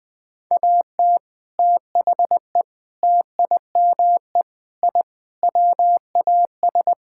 Morse code, heard as a keyed tone, about 20 wpm, 700 Hz. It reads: AT THE TIME I WAS
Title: Sentences with 3-Word Phrases - Spoken Beginning - 20wpm